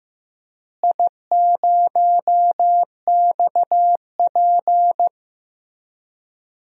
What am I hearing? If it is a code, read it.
Morse code, heard as a keyed tone, about 15 wpm, 700 Hz. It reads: I0XP
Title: DXCC-100 Call Signs - 4 Characters - Repeated 3x - 15wpm